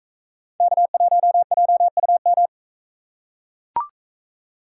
Morse code, heard as a keyed tone, about 44 wpm, 700 Hz. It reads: K1JUM